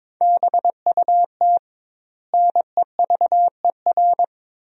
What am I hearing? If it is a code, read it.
Morse code, heard as a keyed tone, about 22 wpm, 700 Hz. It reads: BUT NEVER
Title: 2x Word-Spacing - Sentences from Top 500 Words - 22wpm